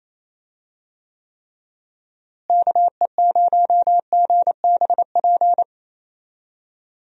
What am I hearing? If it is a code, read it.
Morse code, heard as a keyed tone, about 27 wpm, 700 Hz. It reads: KE0GBP
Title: USA Call Signs - 6 Characters - Repeated 3x - 28wpm